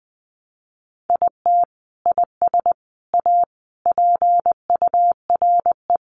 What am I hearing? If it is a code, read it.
Morse code, heard as a keyed tone, about 20 wpm, 700 Hz. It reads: IT IS A PURE
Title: Sentences from Top 1500 Words 20wpm